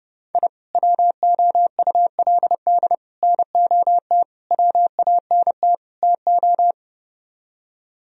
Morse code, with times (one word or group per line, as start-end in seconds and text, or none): 0.3-0.5 I
0.8-3.0 WOULD
3.2-4.2 NOT
4.5-5.8 WANT
6.0-6.7 TO